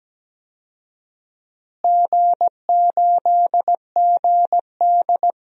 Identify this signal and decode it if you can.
Morse code, heard as a keyed tone, about 17 wpm, 700 Hz. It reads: G8GD
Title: DXCC-100 Call Signs - 4 Characters - Repeated 3x - 17wpm